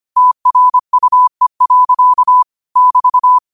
Morse code, heard as a keyed tone, about 25 wpm, 1000 Hz. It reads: TRUE. <BT>